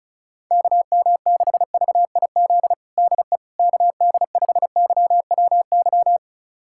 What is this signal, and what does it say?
Morse code, heard as a keyed tone, about 36 wpm, 700 Hz. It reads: KM6VIZ DE KD5YWY
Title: USA Callsign DE Callsign - 6 Characters - 35wpm